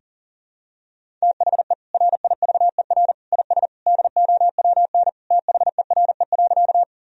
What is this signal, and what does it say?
Morse code, heard as a keyed tone, about 40 wpm, 700 Hz. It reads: THE RIVER IS DOWN THERE.
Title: Sentences from Top 1000 Words - Review Part 1 - 40wpm